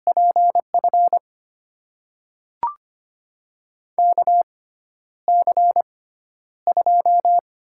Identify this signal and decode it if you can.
Morse code, heard as a keyed tone, about 25 wpm, spaced as for 11 wpm, 700 Hz. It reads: PF KC2
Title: USA Call Signs - 6 Characters - 6x Character Spacing - 25wpm